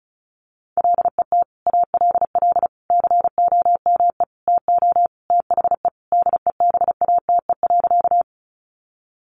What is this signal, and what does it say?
Morse code, heard as a keyed tone, about 34 wpm, 700 Hz. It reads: LET ALL COME TO THE DEBATE.
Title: Sentences from Top 1100 Words 35wpm